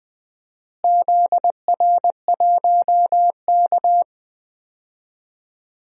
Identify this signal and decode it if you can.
Morse code, heard as a keyed tone, about 20 wpm, 700 Hz. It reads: ZR1K
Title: DXCC-100 Call Signs - 4 Characters - Repeated 2x - 20wpm